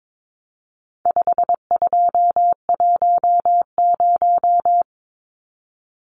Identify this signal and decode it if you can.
Morse code, heard as a keyed tone, about 22 wpm, 700 Hz. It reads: 5210
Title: Numbers - 4-Digit 22wpm